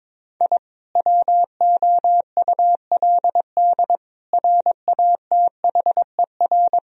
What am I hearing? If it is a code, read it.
Morse code, heard as a keyed tone, about 22 wpm, 700 Hz. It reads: I WOULD RATHER
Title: Sentences from Top 1000 Words - Review Part 4 - 22wpm